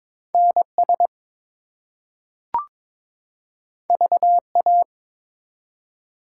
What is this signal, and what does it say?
Morse code, heard as a keyed tone, about 22 wpm, 700 Hz. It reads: NS VA
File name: ARRL Sections v2 - 22wpm